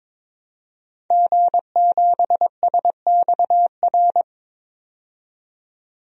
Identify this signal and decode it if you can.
Morse code, heard as a keyed tone, about 21 wpm, 700 Hz. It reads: G7SXR
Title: DXCC-100 Call Signs - 5 Characters - 22wpm